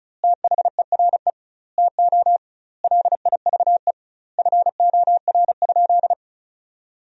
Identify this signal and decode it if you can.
Morse code, heard as a keyed tone, about 36 wpm, 700 Hz. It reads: THERE TO LIVE FOR?
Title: 2x Word-Spacing - Sentences from Top 200 Words - 35wpm